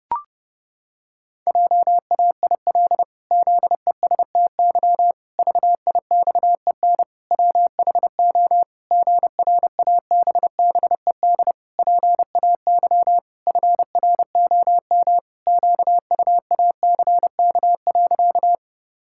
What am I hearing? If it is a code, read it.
Morse code, heard as a keyed tone, about 32 wpm, 700 Hz. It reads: JAIL ZESTY VIXEN WHO GRABBED PAY FROM QUACK.
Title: Pangram Challenge v2 - 30wpm